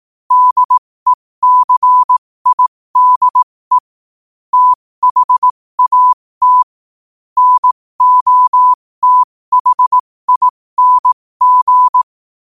Morse code, heard as a keyed tone, about 18 wpm, spaced as for 15 wpm, 1000 Hz. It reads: DECIDE THAT NOTHING
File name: qotdfiles / 15wpm